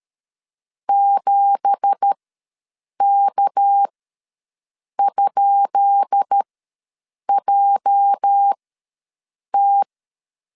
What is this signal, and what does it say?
Morse code, heard as a keyed tone, about 13 wpm, spaced as for 7 wpm, 800 Hz. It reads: 7K?JT
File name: LEZIONI / 10 SIGNS / 10B INT ws 12wpm 3527